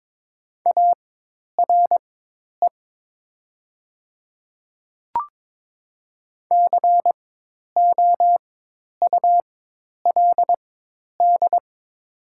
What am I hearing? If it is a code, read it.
Morse code, heard as a keyed tone, about 23 wpm, spaced as for 10 wpm, 700 Hz. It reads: ARE COULD
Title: 4x Character Spacing - Top 300 Words - 22wpm